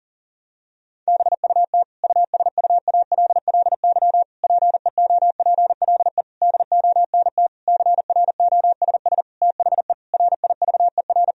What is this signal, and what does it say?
Morse code, heard as a keyed tone, about 40 wpm, 700 Hz. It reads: BUT USUALLY PEOPLE DONT CROSS THE RIVER
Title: Sentences from Top 1000 Words - Review Part 3 - 40wpm